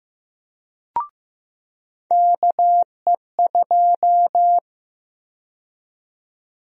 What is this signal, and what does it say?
Morse code, heard as a keyed tone, about 15 wpm, 700 Hz. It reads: KE2
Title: USA Call Sign Prefixes - 15wpm